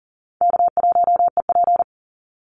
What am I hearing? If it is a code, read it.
Morse code, heard as a keyed tone, about 40 wpm, 700 Hz. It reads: K1EP
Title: Easy Call Signs - 40wpm